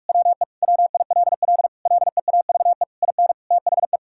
Morse code, heard as a keyed tone, about 42 wpm, 700 Hz. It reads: WE WILL LEAVE IN THE
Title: Sentences from Top 1000 Words - Review Part 6 - 45wpm